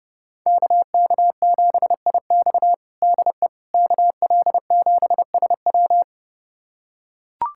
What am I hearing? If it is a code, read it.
Morse code, heard as a keyed tone, about 30 wpm, 700 Hz. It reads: KK7IX DE KL7SW
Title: USA Callsign DE Callsign - 5 Characters - 30wpm